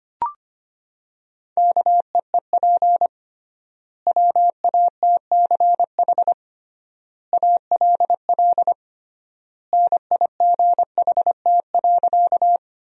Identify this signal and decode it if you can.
Morse code, heard as a keyed tone, about 25 wpm, 700 Hz. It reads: KEEP WATCH ALL NIGHT.
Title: 3x Word-Spacing - Sentences from Top 300 Words - 25wpm